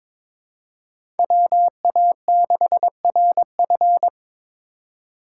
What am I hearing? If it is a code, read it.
Morse code, heard as a keyed tone, about 22 wpm, 700 Hz. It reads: WA6RF
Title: USA Call Signs - 5 Characters - 3x Character Spacing - 22wpm